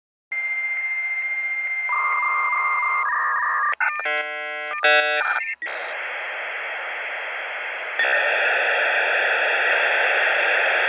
Dial-up modem handshake.